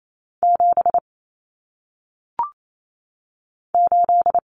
Morse code, heard as a keyed tone, about 28 wpm, 700 Hz. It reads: 7 8
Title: Common Cut Numbers - 1-Digit 28wpm